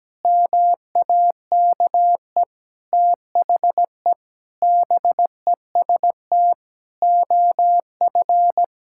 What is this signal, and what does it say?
Morse code, heard as a keyed tone, about 17 wpm, 700 Hz. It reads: MAKE THE BEST OF
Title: Sentences from Top 1000 Words - Review Part 5 - 17wpm